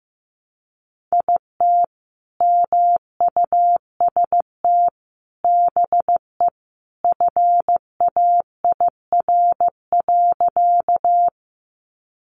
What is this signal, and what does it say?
Morse code, heard as a keyed tone, about 14 wpm, 700 Hz. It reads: IT MUST BE FAIR.
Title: Sentences from Top 1300 Words 15wpm